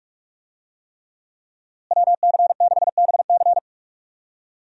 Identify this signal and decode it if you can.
Morse code, heard as a keyed tone, about 46 wpm, 700 Hz. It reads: WC6BC
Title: USA Call Signs - 5 Characters - Repeated 3x - 45wpm